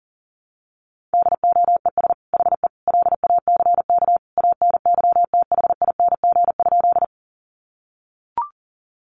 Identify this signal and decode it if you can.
Morse code, heard as a keyed tone, about 40 wpm, 700 Hz. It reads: DOES HE LACK ANYTHING?
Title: Sentences from Top 1100 Words 40wpm